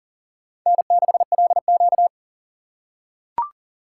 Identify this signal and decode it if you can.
Morse code, heard as a keyed tone, about 40 wpm, 700 Hz. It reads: N6LQ
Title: USA Call Signs - 4 Characters - 6x Character Spacing - 40wpm